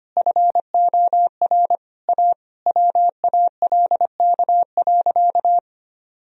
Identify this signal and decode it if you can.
Morse code, heard as a keyed tone, about 25 wpm, 700 Hz. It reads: FOR A WALK.
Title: Sentences from Top 1000 Words - Review Part 1 - 25wpm